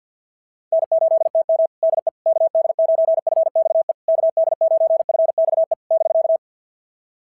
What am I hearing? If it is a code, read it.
Morse code, heard as a keyed tone, about 45 wpm, 650 Hz. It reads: N8TM DE KD9FXE KD9FXE <BK>
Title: SOTA QSO Practice 50wpm